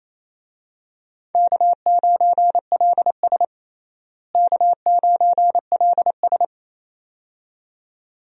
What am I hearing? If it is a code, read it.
Morse code, heard as a keyed tone, about 27 wpm, 700 Hz. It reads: K9LS K9LS
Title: USA Call Signs - 4 Characters - Repeated 2x - 28wpm